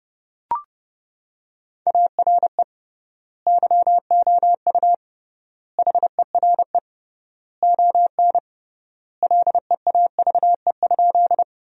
Morse code, heard as a keyed tone, about 30 wpm, 700 Hz. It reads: ARE YOU HERE ON LEAVE?
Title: 3x Word-Spacing - Sentences from Top 400 Words - 30wpm